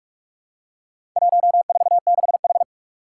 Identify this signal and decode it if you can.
Morse code, heard as a keyed tone, about 44 wpm, 700 Hz. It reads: 146H
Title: Four Challenging Letter-Number Combinations 45wpm